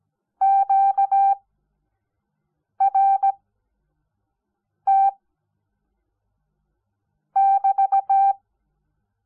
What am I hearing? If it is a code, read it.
Morse code, heard as a keyed tone, about 17 wpm, spaced as for 5 wpm, 780 Hz. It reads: QRT <BT>
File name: Onlinelehrgang / morsen / DJ4UF Morsen05